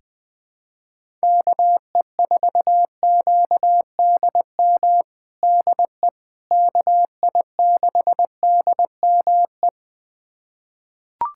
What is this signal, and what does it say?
Morse code, heard as a keyed tone, about 21 wpm, 700 Hz. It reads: KE4QDM DE KI6DME
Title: USA Callsign DE Callsign - 6 Characters - 20wpm